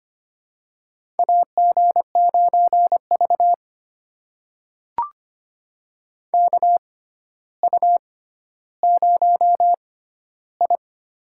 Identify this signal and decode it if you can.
Morse code, heard as a keyed tone, about 25 wpm, spaced as for 13 wpm, 700 Hz. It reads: AG9V KU0I
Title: USA Call Signs - 4 Characters - 6x Character Spacing - 25wpm